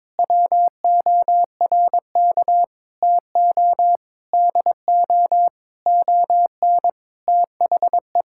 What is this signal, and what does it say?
Morse code, heard as a keyed tone, about 22 wpm, 700 Hz. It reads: WORK TO DO ON THE
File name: Sentences with 9-Word Phrases - 22wpm